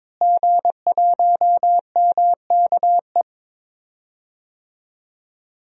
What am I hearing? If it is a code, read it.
Morse code, heard as a keyed tone, about 22 wpm, 700 Hz. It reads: G1MKE
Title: DXCC-100 Call Signs - 5 Characters - Repeated 3x - 22wpm